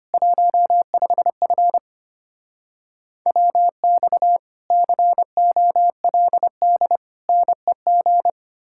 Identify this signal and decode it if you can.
Morse code, heard as a keyed tone, about 33 wpm, 700 Hz. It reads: 15F WX COLD NEG